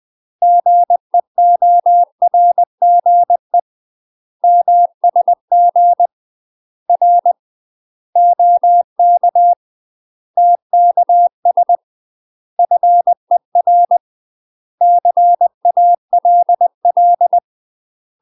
Morse code, heard as a keyed tone, about 20 wpm, 700 Hz. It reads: GEORGE MSG R OK TKS FER CALL
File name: UK MASTER QSOs TRAINING / 20WPM / UK MASTER QSOs 0040